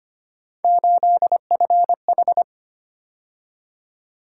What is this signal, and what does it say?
Morse code, heard as a keyed tone, about 25 wpm, 700 Hz. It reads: 8FH
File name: Three Letter-Number Combinations - 25wpm